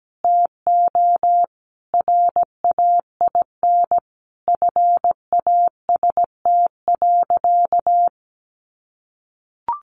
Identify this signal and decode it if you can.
Morse code, heard as a keyed tone, about 17 wpm, 700 Hz. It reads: TO RAIN FAST.
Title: Sentences from Top 800 Words 17wpm